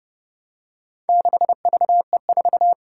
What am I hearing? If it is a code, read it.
Morse code, heard as a keyed tone, about 30 wpm, 700 Hz. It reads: 6VE4
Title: Four Challenging Letter-Number Combinations 30wpm